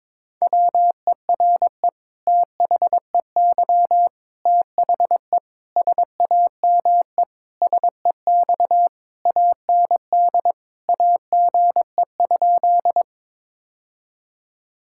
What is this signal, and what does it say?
Morse code, heard as a keyed tone, about 22 wpm, 700 Hz. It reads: WERE THEY THE SAME SEX AND AGE?
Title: Sentences from Top 1600 Words 22wpm